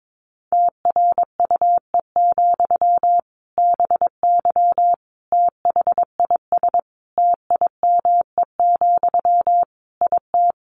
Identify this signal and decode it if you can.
Morse code, heard as a keyed tone, about 22 wpm, 700 Hz. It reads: TRUE, BY THIS TIME, IT
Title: Sentences from Top 1500 Words 22wpm